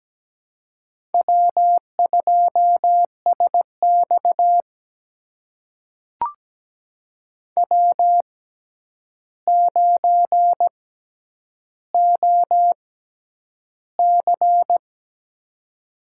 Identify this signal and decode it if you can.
Morse code, heard as a keyed tone, about 17 wpm, spaced as for 9 wpm, 700 Hz. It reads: W2SX W9OC
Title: USA Call Signs - 4 Characters - 6x Character Spacing - 17wpm